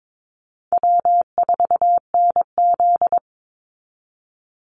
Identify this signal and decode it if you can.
Morse code, heard as a keyed tone, about 22 wpm, 700 Hz. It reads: W4NZ
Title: Easy Call Signs Repeated 3x - 22wpm